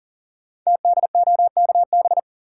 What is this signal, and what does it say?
Morse code, heard as a keyed tone, about 38 wpm, 700 Hz. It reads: TDOKB
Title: Code Groups - 5 Letters - 40wpm